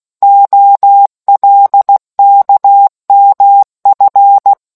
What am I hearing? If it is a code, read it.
Morse code, heard as a keyed tone, about 16 wpm, 800 Hz. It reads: OLKMF